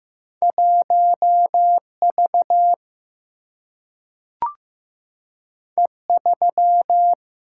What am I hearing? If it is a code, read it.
Morse code, heard as a keyed tone, about 15 wpm, 700 Hz. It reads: 1 V E 3
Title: Single-Letter Single-Number 15wpm